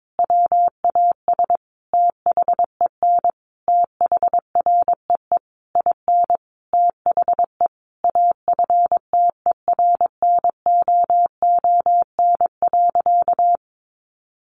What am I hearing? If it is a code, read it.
Morse code, heard as a keyed tone, about 22 wpm, 700 Hz. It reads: WAS THEN THREE IN THE AFTERNOON.
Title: Sentences from Top 1500 Words 22wpm